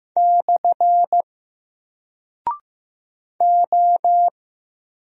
Morse code, heard as a keyed tone, about 15 wpm, 700 Hz. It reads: / O
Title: Single Character (Letter:Number:Punctuation:Prosign) 15wpm